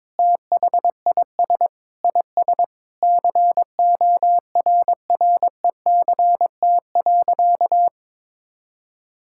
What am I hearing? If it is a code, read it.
Morse code, heard as a keyed tone, about 22 wpm, 700 Hz. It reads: THIS IS CORRECT.
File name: practice / 22wpm / Sentences with 8-Word Phrases - 22wpm